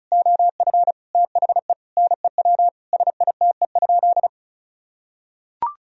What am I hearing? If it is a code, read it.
Morse code, heard as a keyed tone, about 36 wpm, 700 Hz. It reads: OF THE NEW SITE?
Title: Sentences from Top 1000 Words - Review Part 4 - 35wpm